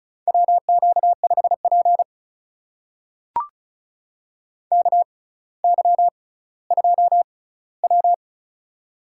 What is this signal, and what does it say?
Morse code, heard as a keyed tone, about 36 wpm, spaced as for 18 wpm, 700 Hz. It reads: WQ5P KY2W
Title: USA Call Signs - 6x Character Spacing - 35wpm